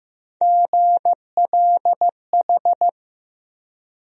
Morse code, heard as a keyed tone, about 15 wpm, 700 Hz. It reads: GLH
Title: USA Call Sign Postfixes - 3 Letters 15wpm